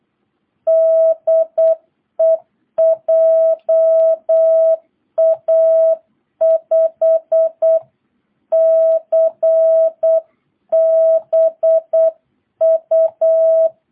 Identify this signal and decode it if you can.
Morse code, heard as a keyed tone, about 8 wpm, 660 Hz. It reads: DE JA5CBU